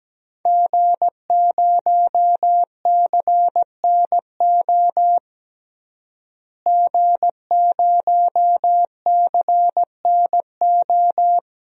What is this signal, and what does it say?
Morse code, heard as a keyed tone, about 17 wpm, 700 Hz. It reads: G0CNO G0CNO
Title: DXCC-100 Call Signs - 5 Characters - Repeated 3x - 17wpm